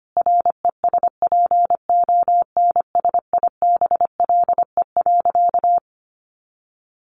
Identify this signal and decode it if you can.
Morse code, heard as a keyed tone, about 25 wpm, 700 Hz. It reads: RESPONSIBLE.
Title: A Sound Of Thunder 25wpm